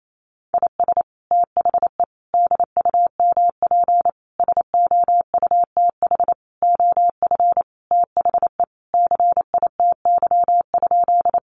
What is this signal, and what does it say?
Morse code, heard as a keyed tone, about 28 wpm, 700 Hz. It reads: IS THE DUMP SOUTH OF THE CITY?
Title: Sentences from Top 1300 Words 28wpm